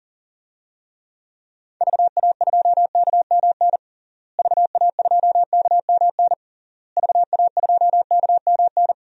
Morse code, heard as a keyed tone, about 40 wpm, 700 Hz. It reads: VA2KMN VA2KMN VA2KMN
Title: DXCC-100 Call Signs - 6 Characters - Repeated 3x - 40wpm